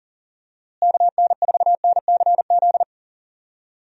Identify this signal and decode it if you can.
Morse code, heard as a keyed tone, about 37 wpm, 700 Hz. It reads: KN4NCZ
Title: USA Call Signs - 6 Characters - Repeated 3x - 40wpm